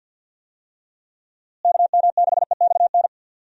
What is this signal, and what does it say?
Morse code, heard as a keyed tone, about 45 wpm, 700 Hz. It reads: KM6EXN
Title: USA Call Signs - 6 Characters - Repeated 2x - 50wpm